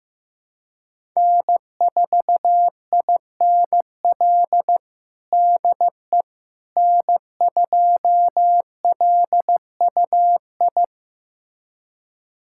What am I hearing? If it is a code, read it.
Morse code, heard as a keyed tone, about 15 wpm, 700 Hz. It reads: N4INL DE N2LUI
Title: USA Callsign DE Callsign - 5 Characters - 15wpm